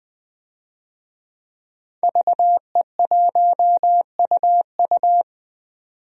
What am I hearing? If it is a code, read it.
Morse code, heard as a keyed tone, about 20 wpm, 700 Hz. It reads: VE1UU
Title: DXCC-100 Call Signs - 5 Characters - 20wpm